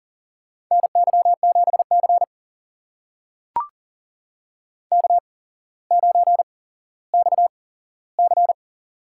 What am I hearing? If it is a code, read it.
Morse code, heard as a keyed tone, about 40 wpm, spaced as for 17 wpm, 700 Hz. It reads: NY7C K9XC
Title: USA Call Signs - 4 Characters - 8x Character Spacing - 40wpm